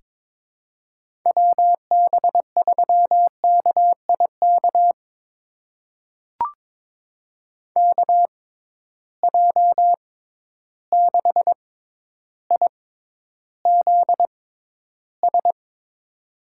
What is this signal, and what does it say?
Morse code, heard as a keyed tone, about 22 wpm, spaced as for 11 wpm, 700 Hz. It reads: WB3KIK KJ6IZS
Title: USA Call Signs - 6 Characters - 6x Character Spacing - 22wpm